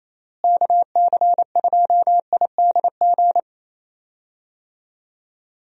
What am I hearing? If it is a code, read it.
Morse code, heard as a keyed tone, about 28 wpm, 700 Hz. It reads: KC2IDG
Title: USA Call Signs - 6 Characters - Repeated 2x - 28wpm